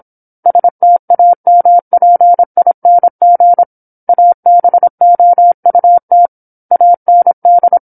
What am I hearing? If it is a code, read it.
Morse code, heard as a keyed tone, about 27 wpm, 700 Hz. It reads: STAMPING ABOUT AND
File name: 08-Chapter-08 26 WPM